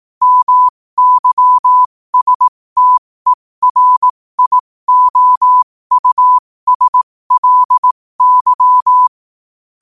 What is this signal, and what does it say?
Morse code, heard as a keyed tone, about 18 wpm, spaced as for 15 wpm, 1000 Hz. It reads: MYSTERIOUSLY